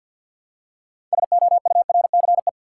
Morse code, heard as a keyed tone, about 53 wpm, 700 Hz. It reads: SOURCE
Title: Six Letter Words - 50wpm